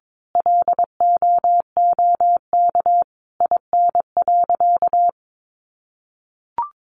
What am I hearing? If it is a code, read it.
Morse code, heard as a keyed tone, about 22 wpm, 700 Hz. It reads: LOOK IN.
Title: Sentences from Top 1300 Words 22wpm